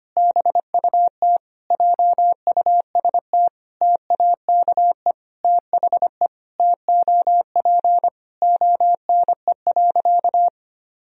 Morse code, heard as a keyed tone, about 25 wpm, 700 Hz. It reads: BUT JUST TAKE THE TOP ONE.
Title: Sentences from Top 1000 Words - Review Part 2 - 25wpm